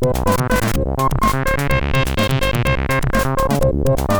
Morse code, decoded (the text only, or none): none